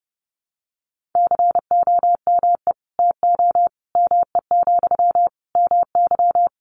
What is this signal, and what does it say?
Morse code, heard as a keyed tone, about 30 wpm, 700 Hz. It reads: COME TO ME, MY
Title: Sentences from Top 1500 Words 30wpm